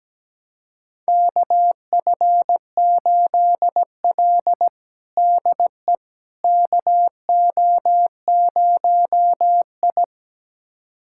Morse code, heard as a keyed tone, about 17 wpm, 700 Hz. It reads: KF8L DE KO0I
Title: USA Callsign DE Callsign - 4 Characters - 17wpm